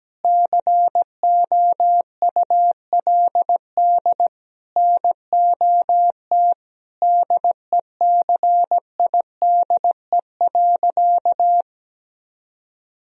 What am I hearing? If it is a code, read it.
Morse code, heard as a keyed tone, about 17 wpm, 700 Hz. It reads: COULD NOT DECIDE.
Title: Sentences from Top 1000 Words - Review Part 3 - 17wpm